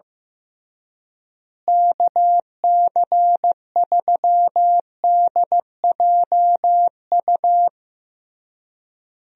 Morse code, heard as a keyed tone, about 15 wpm, 700 Hz. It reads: KC3DJU
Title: USA Call Signs - 6 Characters - Repeated 3x - 15wpm